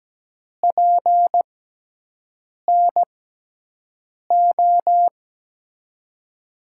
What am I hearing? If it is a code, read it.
Morse code, heard as a keyed tone, about 17 wpm, spaced as for 6 wpm, 700 Hz. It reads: PNO
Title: USA Call Signs - 6 Characters - 6x Character Spacing - 17wpm